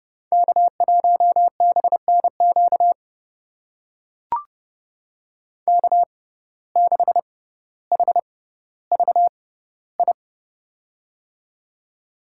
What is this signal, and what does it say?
Morse code, heard as a keyed tone, about 30 wpm, spaced as for 15 wpm, 700 Hz. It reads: K1BNQ K6HVI
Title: USA Call Signs - 5 Characters - 6x Character Spacing - 30wpm